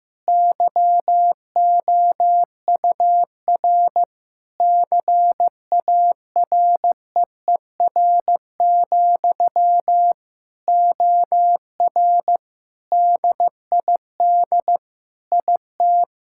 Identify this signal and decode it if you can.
Morse code, heard as a keyed tone, about 15 wpm, 700 Hz. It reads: YOUR CAREER, OR DID IT
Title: Sentences from Top 1000 Words - Review Part 6 - 15wpm